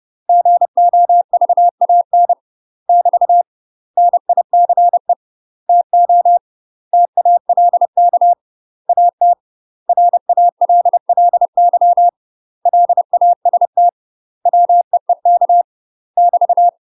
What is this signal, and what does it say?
Morse code, heard as a keyed tone, about 30 wpm, 700 Hz. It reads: GOVAN <BT> NICE TO TALK AT RALLY LAST WEEK <BT>
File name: UK MASTER QSOs TRAINING / 30WPM / UK MASTER QSOs 0043